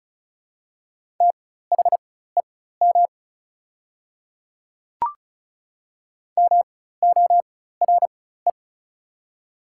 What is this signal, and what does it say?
Morse code, heard as a keyed tone, about 34 wpm, spaced as for 16 wpm, 700 Hz. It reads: THEM MORE
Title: 4x Character Spacing - Top 100 Words - 35wpm